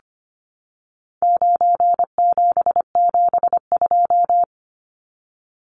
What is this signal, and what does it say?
Morse code, heard as a keyed tone, about 25 wpm, 700 Hz. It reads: 9772
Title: Numbers - 4-Digit 25wpm